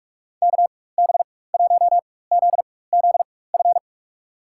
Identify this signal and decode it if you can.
Morse code, heard as a keyed tone, about 42 wpm, spaced as for 21 wpm, 700 Hz. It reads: KB1ZZF
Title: USA Call Signs - 6 Characters - 4x Character Spacing - 45wpm